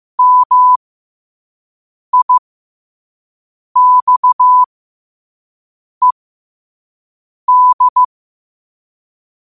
Morse code, heard as a keyed tone, about 15 wpm, spaced as for 5 wpm, 1000 Hz. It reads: MIXED
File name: qotdfiles / 5wpm